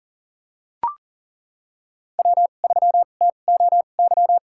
Morse code, heard as a keyed tone, about 38 wpm, spaced as for 28 wpm, 700 Hz. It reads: W3TOY
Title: USA Call Signs - 2x Character Spacing - 40wpm